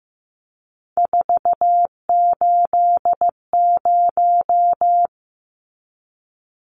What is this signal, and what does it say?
Morse code, heard as a keyed tone, about 15 wpm, 700 Hz. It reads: 480
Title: Numbers - 3-Digit 15wpm